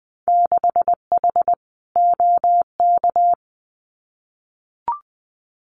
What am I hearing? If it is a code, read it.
Morse code, heard as a keyed tone, about 20 wpm, 700 Hz. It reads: 6H OK
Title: Winter Field Day Exchanges Encore 20wpm